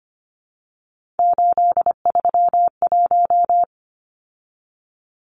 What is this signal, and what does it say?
Morse code, heard as a keyed tone, about 25 wpm, 700 Hz. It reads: 831
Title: Numbers - 3-Digit 25wpm